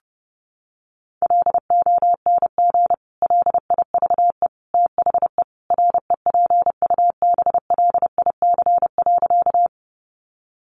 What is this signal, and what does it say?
Morse code, heard as a keyed tone, about 30 wpm, 700 Hz. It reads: LONG LIVE THE REPUBLIC.
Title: Sentences from Top 1500 Words 30wpm